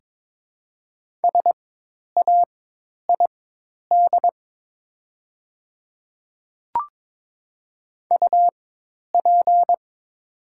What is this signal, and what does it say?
Morse code, heard as a keyed tone, about 23 wpm, spaced as for 10 wpm, 700 Hz. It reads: SAID UP
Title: 4x Character Spacing - Top 100 Words - 22wpm